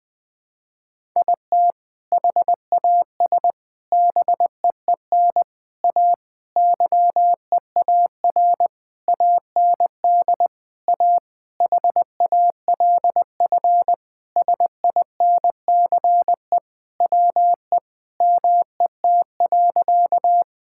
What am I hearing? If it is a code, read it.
Morse code, heard as a keyed tone, about 20 wpm, 700 Hz. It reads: IT HAS BEEN A YEAR AND A HALF SINCE WE MET.
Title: Sentences with 4-Word Phrases - 20wpm